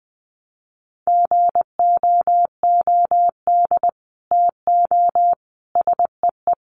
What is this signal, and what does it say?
Morse code, heard as a keyed tone, about 20 wpm, 700 Hz. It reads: GOOD TO SEE
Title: Sets of 5 Words Encore2 20wpm